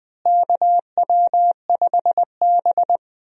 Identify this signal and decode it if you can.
Morse code, heard as a keyed tone, about 20 wpm, 700 Hz. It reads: KW5B
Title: USA Call Signs - 4 Characters - Repeated 3x - 20wpm